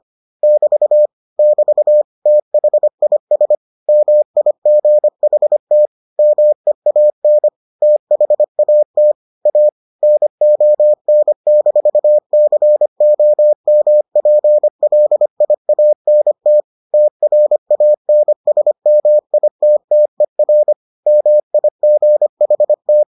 Morse code, heard as a keyed tone, about 25 wpm, 600 Hz. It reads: <BT> <BT> THIS MIGHT MEAN THAT A NON-COMPLIANT TRANSMITTER MIGHT